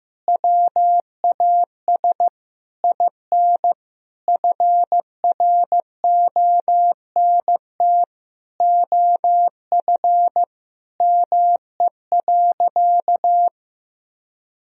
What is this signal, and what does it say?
Morse code, heard as a keyed tone, about 15 wpm, 700 Hz. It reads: WAS IN FRONT OF ME.
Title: Sentences from Top 1000 Words - Review Part 2 - 15wpm